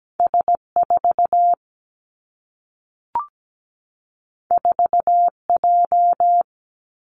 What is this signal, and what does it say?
Morse code, heard as a keyed tone, about 17 wpm, 700 Hz. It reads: S4 4J
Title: Two Challenging Letter-Number Combinations 17wpm